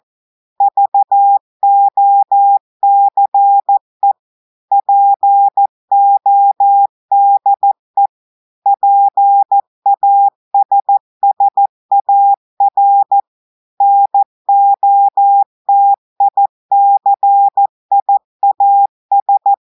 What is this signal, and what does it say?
Morse code, heard as a keyed tone, about 14 wpm, 800 Hz. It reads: VOCE PODE PASSAR NOTICIAS
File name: audios / portugais / portugais texte en clair 2 vitesse 14 mots minutes